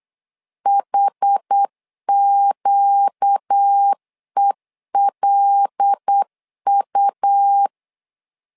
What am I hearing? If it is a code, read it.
Morse code, heard as a keyed tone, about 8 wpm, 800 Hz. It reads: HQELU
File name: LEZIONI / 5 LBVC / 5 LB 8wpm 2816